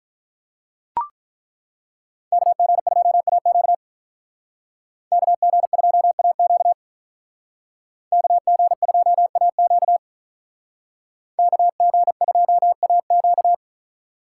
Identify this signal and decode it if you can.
Morse code, heard as a keyed tone, about 42 wpm, 700 Hz. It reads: KG2AQ KG2AQ KG2AQ KG2AQ
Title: Speed Racing - USA Call Signs 35wpm